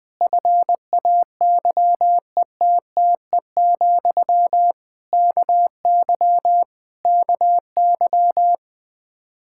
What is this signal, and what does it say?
Morse code, heard as a keyed tone, about 20 wpm, 700 Hz. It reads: FAYETTE, KY KY
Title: Practice QSO - Exchange 1 - Beginning - 20wpm